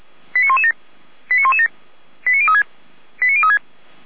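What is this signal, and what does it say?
Five-tone selective call (ZVEI), heard as quick runs of five tones, tones of about 80 ms. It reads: 89117 89117 89036 89036
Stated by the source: ZVEI five-tone call received on an SDR.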